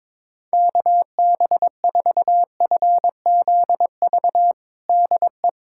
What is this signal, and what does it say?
Morse code, heard as a keyed tone, about 21 wpm, 700 Hz. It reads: KB4FZV DE
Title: USA Callsign DE Callsign - 6 Characters - 22wpm